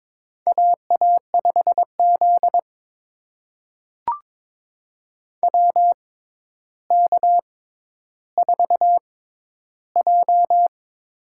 Morse code, heard as a keyed tone, about 23 wpm, spaced as for 11 wpm, 700 Hz. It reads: AA5Z WK4J
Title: USA Call Signs - 4 Characters - 6x Character Spacing - 22wpm